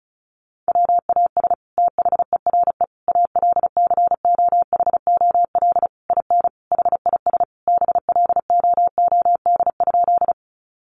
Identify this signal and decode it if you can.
Morse code, heard as a keyed tone, about 34 wpm, 700 Hz. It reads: WAS THERE ALCOHOL IN HIS BLOOD?
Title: Sentences from Top 1400 Words 35wpm